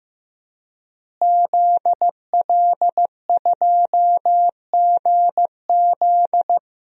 Morse code, heard as a keyed tone, about 15 wpm, 700 Hz. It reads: ZL2GZ
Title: DXCC-100 Call Signs - 5 Characters - Repeated 3x - 15wpm